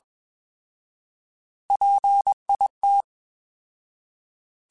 Morse code, heard as a keyed tone, about 21 wpm, 800 Hz. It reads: PIT
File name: Alphabetical / Alphabetical mp3s / Ps @20